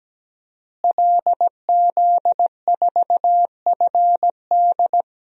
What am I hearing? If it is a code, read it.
Morse code, heard as a keyed tone, about 17 wpm, 700 Hz. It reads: LZ4FD
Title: DXCC-100 Call Signs - 5 Characters - Repeated 3x - 17wpm